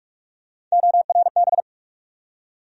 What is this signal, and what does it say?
Morse code, heard as a keyed tone, about 46 wpm, 700 Hz. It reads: ORB
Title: USA Call Sign Postfixes - 3 Letters 45wpm